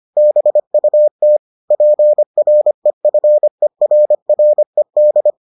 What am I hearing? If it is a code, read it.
Morse code, heard as a keyed tone, about 25 wpm, 600 Hz. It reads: BUT PREFERRED